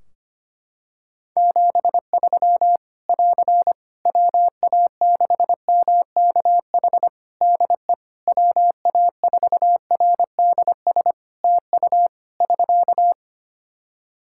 Morse code, heard as a keyed tone, about 25 wpm, 700 Hz. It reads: 73 <AR> WA6MKH DE WA4RDS TU <SK>
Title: Practice QSO - Exchange 4 - Ending - 25wpm